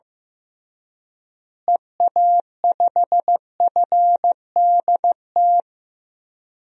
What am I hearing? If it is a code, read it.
Morse code, heard as a keyed tone, about 15 wpm, 700 Hz. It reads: EA5FDT
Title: DXCC-100 Call Signs - 6 Characters - Repeated 2x - 15wpm